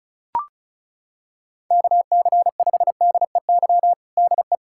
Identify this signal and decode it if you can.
Morse code, heard as a keyed tone, about 34 wpm, 700 Hz. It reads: KC5DEY DE
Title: K1USN SST QSO Practice 35wpm